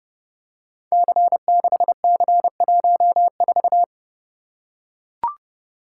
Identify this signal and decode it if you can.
Morse code, heard as a keyed tone, about 28 wpm, 700 Hz. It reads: C6C14
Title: Code Groups - 5 Letter:Numbers - 30wpm